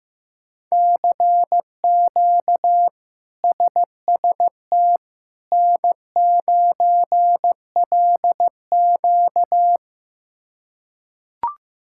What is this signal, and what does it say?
Morse code, heard as a keyed tone, about 15 wpm, 700 Hz. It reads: CQ SST N9LQ
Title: K1USN SST QSO Practice - 15wpm